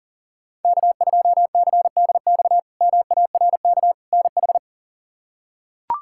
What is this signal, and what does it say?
Morse code, heard as a keyed tone, about 37 wpm, 700 Hz. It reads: K2CDX MARK NH
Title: K1USN SST QSO Practice - 40wpm